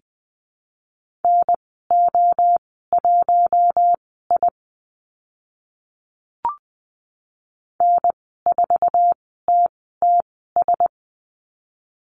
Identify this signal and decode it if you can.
Morse code, heard as a keyed tone, about 20 wpm, spaced as for 14 wpm, 700 Hz. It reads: NO1I N4TTS
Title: 048 Lesson 9g - Number 4 - Call Signs 20wpm